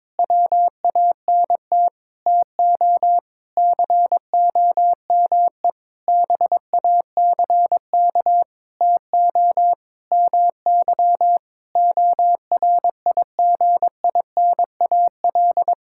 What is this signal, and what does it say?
Morse code, heard as a keyed tone, about 21 wpm, 700 Hz. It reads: WANT TO COME BACK TO MY ORIGINAL
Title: Sentences from Top 1000 Words - Review Part 6 - 22wpm